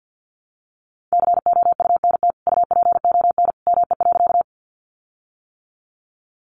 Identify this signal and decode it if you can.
Morse code, heard as a keyed tone, about 50 wpm, 700 Hz. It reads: COUNT UPON ME.
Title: Sentences from Top 1500 Words 50wpm